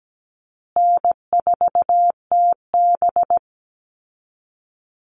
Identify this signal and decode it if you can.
Morse code, heard as a keyed tone, about 17 wpm, 700 Hz. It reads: N4TB
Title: Easy Call Signs Repeated 3x - 17wpm